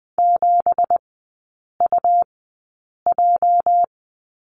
Morse code, heard as a keyed tone, about 20 wpm, spaced as for 8 wpm, 690 Hz. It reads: 7UJ
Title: 205 Lesson 33e - Letter j - Three Characters 20wpm